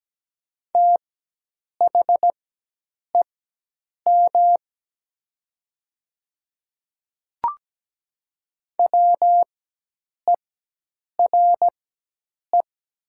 Morse code, heard as a keyed tone, about 17 wpm, spaced as for 8 wpm, 700 Hz. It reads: THEM WERE
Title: 4x Character Spacing - Top 100 Words - 17wpm